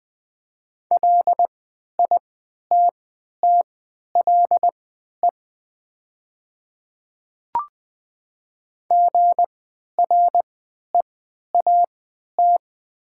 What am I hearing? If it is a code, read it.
Morse code, heard as a keyed tone, about 20 wpm, spaced as for 11 wpm, 700 Hz. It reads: LITTLE GREAT
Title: 3x Character Spacing - Top 200 Words - 20wpm